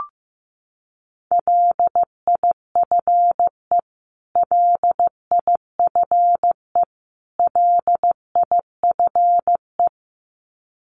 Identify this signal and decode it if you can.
Morse code, heard as a keyed tone, about 15 wpm, 700 Hz. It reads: LIFE LIFE LIFE
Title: Top 500 Words Repeated 15wpm